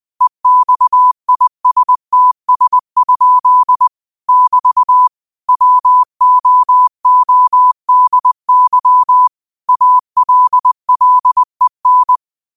Morse code, heard as a keyed tone, about 20 wpm, 1000 Hz. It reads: EXISTS? <BT> WOODY ALLEN